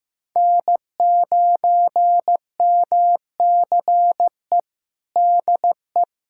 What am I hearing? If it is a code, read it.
Morse code, heard as a keyed tone, about 15 wpm, 700 Hz. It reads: N9MCE DE
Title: USA Callsign DE Callsign - 5 Characters - 15wpm